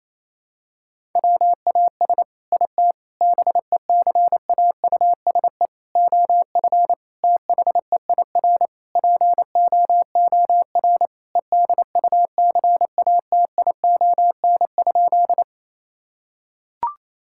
Morse code, heard as a keyed tone, about 28 wpm, 700 Hz. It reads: WAS IT BECAUSE OF THEIR POOR EDUCATION?
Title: Sentences from Top 1000 Words - Review Part 6 - 28wpm